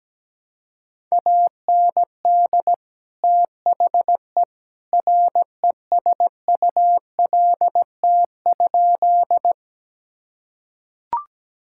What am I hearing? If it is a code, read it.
Morse code, heard as a keyed tone, about 17 wpm, 700 Hz. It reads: AND THE RESULT?
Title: Sentences from Top 1000 Words - Review Part 5 - 17wpm